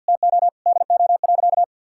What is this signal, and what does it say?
Morse code, heard as a keyed tone, about 52 wpm, 700 Hz. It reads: TO DO.
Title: Sentences from Top 1000 Words - Review Part 5 - 50wpm